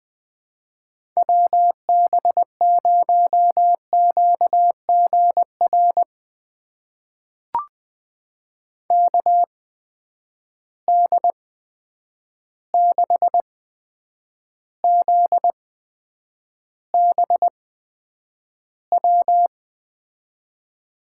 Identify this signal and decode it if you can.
Morse code, heard as a keyed tone, about 20 wpm, spaced as for 9 wpm, 700 Hz. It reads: WB0QGR KD6ZBW
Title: USA Call Signs - 6 Characters - 8x Character Spacing - 20wpm